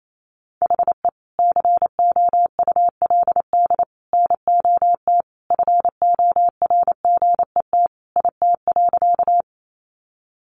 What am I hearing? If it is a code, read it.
Morse code, heard as a keyed tone, about 28 wpm, 700 Hz. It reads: HE COULD NOT FORGET IT.
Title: Sentences from Top 1200 Words 28wpm